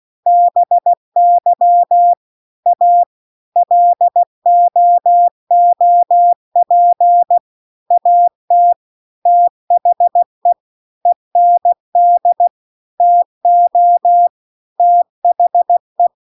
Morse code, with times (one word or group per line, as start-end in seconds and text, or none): 0.3-2.1 BY
2.7-3.0 A
3.6-7.4 LOOP
7.9-8.7 AT
9.3-10.5 THE
11.1-12.5 END
13.0-14.3 TO
14.8-16.1 THE